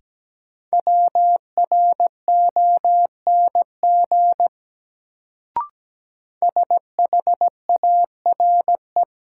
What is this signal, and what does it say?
Morse code, heard as a keyed tone, about 17 wpm, 700 Hz. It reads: WRONG SHARE
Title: Sentences from Top 900 Words - New Word Drill - 17wpm